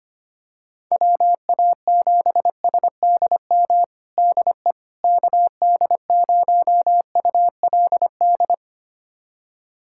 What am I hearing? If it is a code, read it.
Morse code, heard as a keyed tone, about 25 wpm, 700 Hz. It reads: WA7SDM DE KD0ULD
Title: USA Callsign DE Callsign - 6 Characters - 25wpm